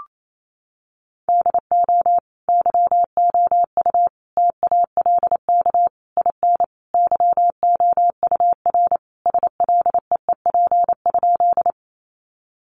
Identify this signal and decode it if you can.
Morse code, heard as a keyed tone, about 28 wpm, 700 Hz. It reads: DO YOU TALK IN YOUR SLEEP?
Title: Sentences from Top 500 Words 28wpm